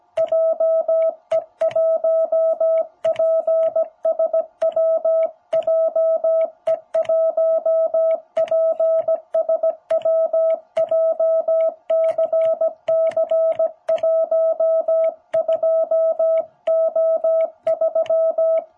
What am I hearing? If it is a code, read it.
Morse code, heard as a keyed tone, about 17 wpm, 660 Hz. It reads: JE1PSW JE1PSW JCC12O3
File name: jh8cbh / 2021cw